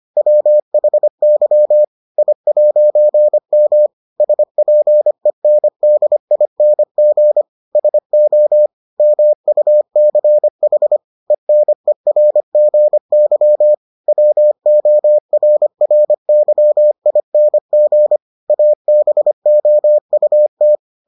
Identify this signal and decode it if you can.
Morse code, heard as a keyed tone, about 25 wpm, 600 Hz. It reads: WHY I'M SPENDING SO MUCH ENERGY WORRYING ABOUT